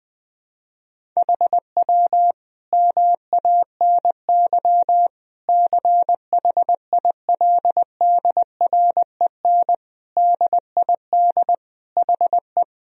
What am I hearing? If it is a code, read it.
Morse code, heard as a keyed tone, about 21 wpm, 700 Hz. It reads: HW MANY CHILDREN DID HE
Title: Sentences from Top 400 Words and Abbreviations 20wpm